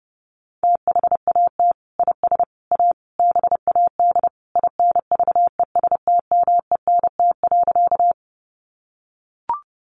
Morse code, heard as a keyed tone, about 30 wpm, 700 Hz. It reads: THAT IS A BAD INVESTMENT.
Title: Sentences from Top 700 Words 30wpm